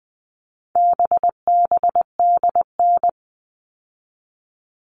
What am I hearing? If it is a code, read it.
Morse code, heard as a keyed tone, about 20 wpm, 700 Hz. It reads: BBDN
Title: All Cut Numbers - 4-Digit 20wpm